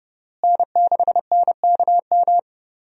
Morse code, heard as a keyed tone, about 29 wpm, 700 Hz. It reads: N6NKM
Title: USA Call Signs - 5 Characters - Repeated 2x - 30wpm